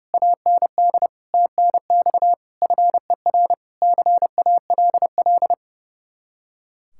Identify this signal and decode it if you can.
Morse code, heard as a keyed tone, about 35 wpm, 700 Hz. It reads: AND TNX FER CALL
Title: Speed-Racing Common QSO Phrases - 30wpm